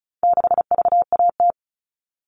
Morse code, heard as a keyed tone, about 34 wpm, 700 Hz. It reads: 6VAT
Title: All Cut Numbers - 4-Digit 35wpm